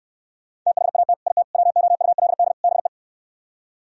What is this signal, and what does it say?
Morse code, heard as a keyed tone, about 68 wpm, 700 Hz. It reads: THAT IT WOULD BE